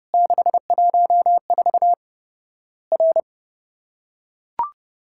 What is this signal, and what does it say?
Morse code, heard as a keyed tone, about 30 wpm, 700 Hz. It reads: 614 R
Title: Worked All Europe DX Contest - QTC Exchange - 30wpm